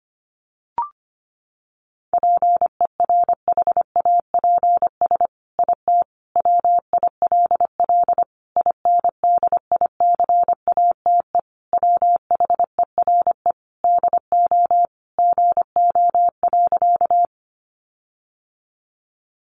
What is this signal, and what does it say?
Morse code, heard as a keyed tone, about 25 wpm, 700 Hz. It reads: PERHAPS IT WILL INDICATE WHERE DO GO.
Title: Sentences from Top 1300 Words 25wpm